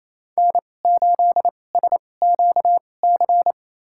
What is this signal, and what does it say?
Morse code, heard as a keyed tone, about 28 wpm, spaced as for 20 wpm, 700 Hz. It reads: N8SQC
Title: USA Call Signs - 5 Characters - 2x Character Spacing - 28wpm